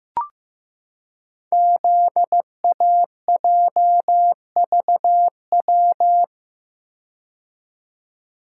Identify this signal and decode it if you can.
Morse code, heard as a keyed tone, about 15 wpm, 700 Hz. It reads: ZAJVW
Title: Code Groups - 5 Letters - 15wpm